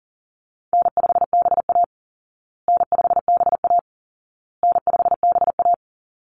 Morse code, heard as a keyed tone, about 40 wpm, 700 Hz. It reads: N5BA N5BA N5BA
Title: Easy Call Signs Repeated 3x - 40wpm